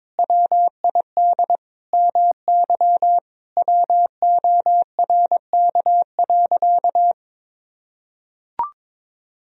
Morse code, heard as a keyed tone, about 21 wpm, 700 Hz. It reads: WID MY WORK.
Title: Sentences from Top 300 Words and Abbreviations 22wpm